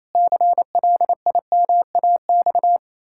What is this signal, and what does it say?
Morse code, heard as a keyed tone, about 28 wpm, 700 Hz. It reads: CLIMAX
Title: Pangram Challenge v2 - 28wpm